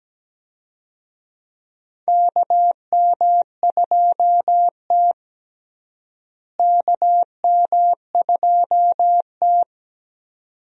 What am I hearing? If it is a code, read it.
Morse code, heard as a keyed tone, about 17 wpm, 700 Hz. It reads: KM2T KM2T
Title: USA Call Signs - 4 Characters - Repeated 2x - 17wpm